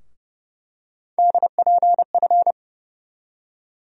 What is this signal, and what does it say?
Morse code, heard as a keyed tone, about 30 wpm, 700 Hz. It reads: DPF
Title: USA Call Sign Postfixes - 3 Letters 30wpm